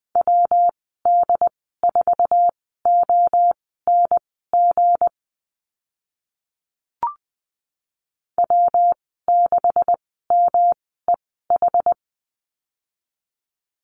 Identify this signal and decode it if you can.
Morse code, heard as a keyed tone, about 20 wpm, spaced as for 14 wpm, 700 Hz. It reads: WD4ONG W6MEH
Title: 161 Lesson 26g - Letter g - Call Signs 20wpm